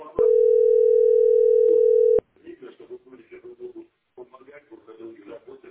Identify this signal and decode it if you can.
Ringback tone (North America), 440 + 480 Hz, with speech.
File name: 6e -e72c-4de3-a55f